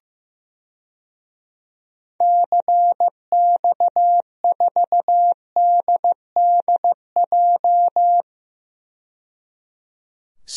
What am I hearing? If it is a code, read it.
Morse code, heard as a keyed tone, about 15 wpm, 700 Hz. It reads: CX4DDJ
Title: DXCC-100 Call Signs - 6 Characters - 15wpm